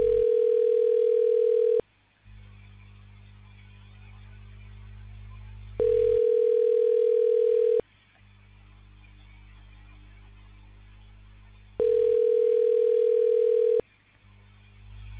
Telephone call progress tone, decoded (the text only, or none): ringback tone